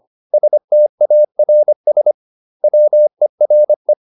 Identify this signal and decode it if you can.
Morse code, heard as a keyed tone, about 25 wpm, 600 Hz. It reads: STARS WERE